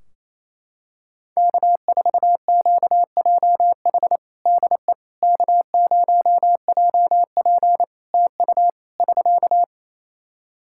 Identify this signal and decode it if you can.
Morse code, heard as a keyed tone, about 28 wpm, 700 Hz. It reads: K4QJH DE K0JP TU <SK>
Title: Practice QSO - Exchange 4 - Ending - 28wpm